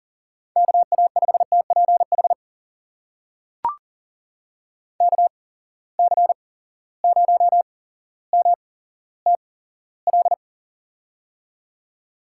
Morse code, heard as a keyed tone, about 39 wpm, spaced as for 17 wpm, 700 Hz. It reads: KA5TPH KC0MTL
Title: USA Call Signs - 6 Characters - 8x Character Spacing - 40wpm